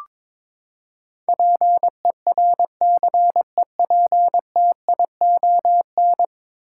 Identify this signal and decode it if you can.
Morse code, heard as a keyed tone, about 22 wpm, 700 Hz. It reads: PERCEPTION
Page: morse-code-ninja-practice-22wpm